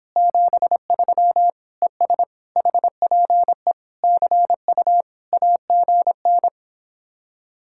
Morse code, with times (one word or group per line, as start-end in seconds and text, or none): 0.2-1.5 73
1.8-2.2 ES
2.6-3.7 HPE
4.0-5.0 CU
5.3-6.5 AGN